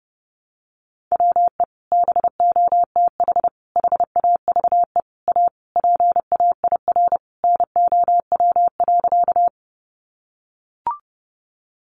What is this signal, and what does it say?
Morse code, heard as a keyed tone, about 30 wpm, 700 Hz. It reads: WE BOTH HAVE A PAIR NOW.
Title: Sentences from Top 1200 Words 30wpm